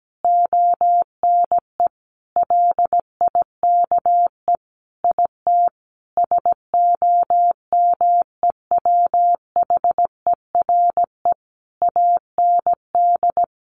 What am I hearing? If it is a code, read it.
Morse code, heard as a keyed tone, about 17 wpm, 700 Hz. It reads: ONE LIKE IT SOMEWHERE AND
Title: Sentences from Top 1400 Words 17wpm